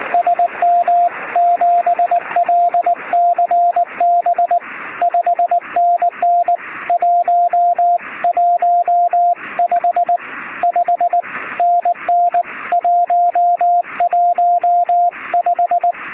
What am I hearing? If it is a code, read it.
Morse code, heard as a keyed tone, about 19 wpm, 700 Hz, with radio noise. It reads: SM7LCB 5NN 115 5NN 115
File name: sm7lcb / 0922z ok1mac 23cm